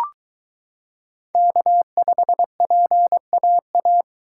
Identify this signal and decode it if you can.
Morse code, heard as a keyed tone, about 22 wpm, 700 Hz. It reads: K5PAA